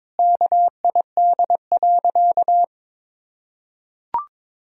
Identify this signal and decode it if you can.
Morse code, heard as a keyed tone, about 22 wpm, 700 Hz. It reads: KID.
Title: Sentences from Top 1600 Words 22wpm